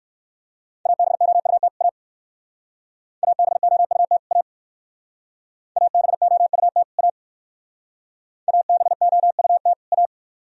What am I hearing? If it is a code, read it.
Morse code, heard as a keyed tone, about 59 wpm, 700 Hz. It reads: ABOUT A ABOUT A ABOUT A ABOUT A